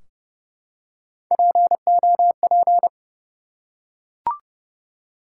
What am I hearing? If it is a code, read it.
Morse code, heard as a keyed tone, about 30 wpm, 700 Hz. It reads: POP
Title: Three Letter Words - 30wpm